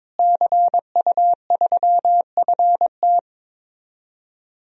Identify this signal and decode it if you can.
Morse code, heard as a keyed tone, about 21 wpm, 700 Hz. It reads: CU3FT
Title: DXCC-100 Call Signs - 5 Characters - Repeated 3x - 22wpm